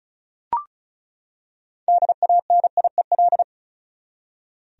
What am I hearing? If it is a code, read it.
Morse code, heard as a keyed tone, about 34 wpm, 700 Hz. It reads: DANIEL
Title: USA Common Names - 35wpm